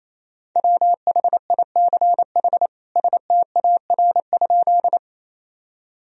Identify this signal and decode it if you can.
Morse code, heard as a keyed tone, about 28 wpm, 700 Hz. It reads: WHICH STAR?
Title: Sentences from Top 1000 Words - Review Part 2 - 28wpm